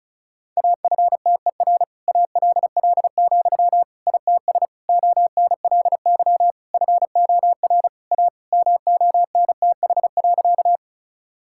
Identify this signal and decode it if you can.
Morse code, heard as a keyed tone, about 36 wpm, 700 Hz. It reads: AFTER ALL, ITS ONLY FOR A MONTH.
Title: Sentences from Top 1000 Words - Review Part 5 - 35wpm